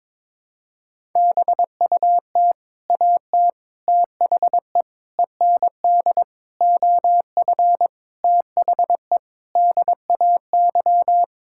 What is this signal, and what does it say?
Morse code, heard as a keyed tone, about 22 wpm, 700 Hz. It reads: BUT AT THE END OF THE DAY
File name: practice / 22wpm / Sets of 7 Words - 22wpm